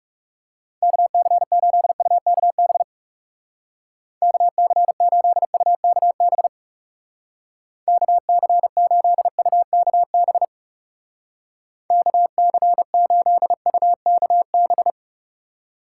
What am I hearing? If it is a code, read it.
Morse code, heard as a keyed tone, about 36 wpm, 700 Hz. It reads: KC8UKB KC8UKB KC8UKB KC8UKB
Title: Speed Racing - USA 2x3 Call Signs 30wpm